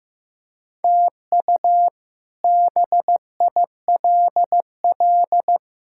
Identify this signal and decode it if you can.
Morse code, heard as a keyed tone, about 15 wpm, 700 Hz. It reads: TU BILL
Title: K1USN SST QSO Practice - 15wpm